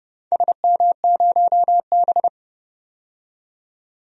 Morse code, heard as a keyed tone, about 30 wpm, 700 Hz. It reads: SM0B
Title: DXCC-100 Call Signs - 4 Characters - 30wpm